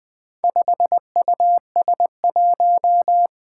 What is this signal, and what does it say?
Morse code, heard as a keyed tone, about 20 wpm, 700 Hz. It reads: 5US1
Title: Four Challenging Letter-Number Combinations 20wpm